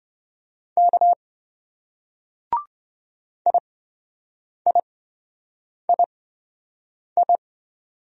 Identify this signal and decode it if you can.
Morse code, heard as a keyed tone, about 28 wpm, spaced as for 7 wpm, 700 Hz. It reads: K IIII